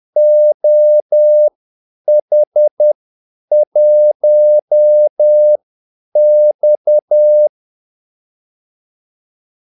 Morse code, heard as a keyed tone, about 10 wpm, spaced as for 8 wpm, 600 Hz. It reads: OH1X
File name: Callsigns-10-8wpm